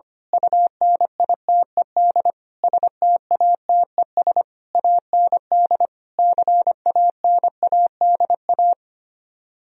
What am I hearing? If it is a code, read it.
Morse code, heard as a keyed tone, about 24 wpm, 700 Hz. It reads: UNITED STATES AND CANADA
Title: Speed-Racing - Sets of 5 Words 25wpm